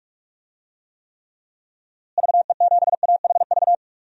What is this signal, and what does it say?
Morse code, heard as a keyed tone, about 46 wpm, 700 Hz. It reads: VE7AHV
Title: DXCC-100 Call Signs - 6 Characters - Repeated 2x - 45wpm